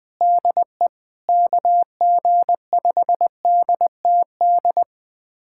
Morse code, heard as a keyed tone, about 20 wpm, 700 Hz. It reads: DE KG5DTD
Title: USA Callsign DE Callsign - 6 Characters - 20wpm